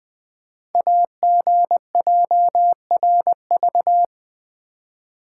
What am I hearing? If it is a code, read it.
Morse code, heard as a keyed tone, about 20 wpm, 700 Hz. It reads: AGJRV
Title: Code Groups - 5 Letter:Numbers - 20wpm